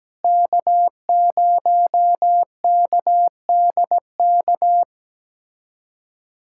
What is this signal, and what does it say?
Morse code, heard as a keyed tone, about 17 wpm, 700 Hz. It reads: K0KDK
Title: USA Callsign DE Callsign - 5 Characters - 17wpm